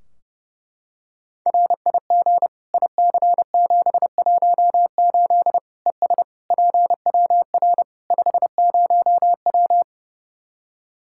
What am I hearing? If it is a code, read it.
Morse code, heard as a keyed tone, about 30 wpm, 700 Hz. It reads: RIG IC718 ES PWR 50W
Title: Practice QSO - Exchange 2 - Info - 30wpm